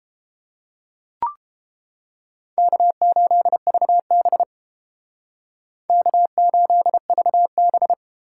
Morse code, heard as a keyed tone, about 31 wpm, 700 Hz. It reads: K8VB K8VB